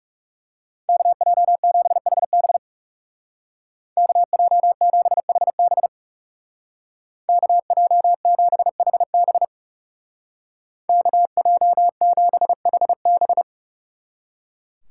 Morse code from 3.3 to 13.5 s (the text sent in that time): KJ7HB KJ7HB KJ7HB